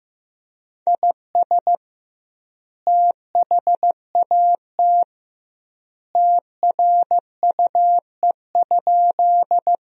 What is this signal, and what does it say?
Morse code, heard as a keyed tone, about 15 wpm, 700 Hz. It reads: IS THAT TRUE?
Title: 2x Word-Spacing - Sentences from Top 400 Words - 15wpm